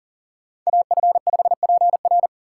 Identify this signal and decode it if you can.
Morse code, heard as a keyed tone, about 40 wpm, 700 Hz. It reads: AF5PR
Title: USA Call Signs - 3x Character Spacing - 40wpm